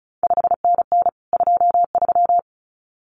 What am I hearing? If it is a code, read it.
Morse code, heard as a keyed tone, about 36 wpm, 700 Hz. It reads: 5NN 23
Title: CQ WW DX Contest Exchange Encore - 35wpm